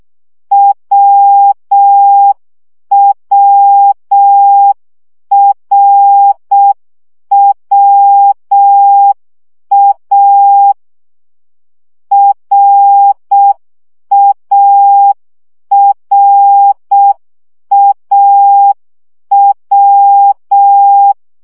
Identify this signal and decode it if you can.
Morse code, heard as a keyed tone, about 6 wpm, 800 Hz. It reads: WWRWA RARAW
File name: CW gyakorlas a r w 5 baud